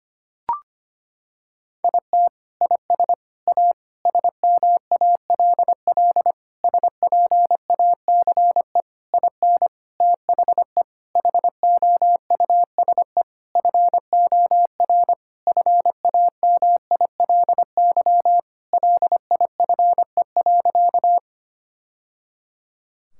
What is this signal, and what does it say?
Morse code, heard as a keyed tone, about 25 wpm, 700 Hz. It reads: IT IS A SMALL SPACE IN THE HOUSE FOR FAMILY LIFE.
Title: Sentences from Top 1000 Words - Review Part 5 - 25wpm